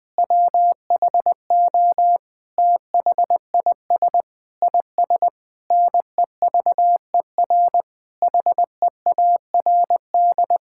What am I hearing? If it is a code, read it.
Morse code, heard as a keyed tone, about 20 wpm, 700 Hz. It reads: WHO THIS IS NEVER HEARD
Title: Sentences with 9-Word Phrases - 20wpm